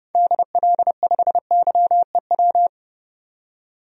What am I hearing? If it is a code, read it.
Morse code, heard as a keyed tone, about 32 wpm, 700 Hz. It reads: DL5YEW
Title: DXCC-100 Call Signs - 6 Characters - 30wpm